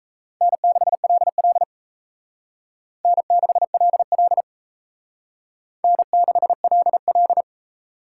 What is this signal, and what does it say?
Morse code, heard as a keyed tone, about 35 wpm, 700 Hz. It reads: N6LL N6LL N6LL
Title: Speed Racing - USA 1x2 Call Signs 28wpm